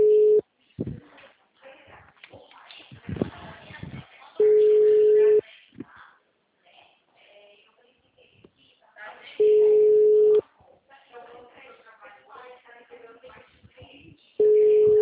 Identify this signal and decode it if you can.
Ringback tone (Europe, 425 Hz), with speech.
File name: out- -2012- - - .58211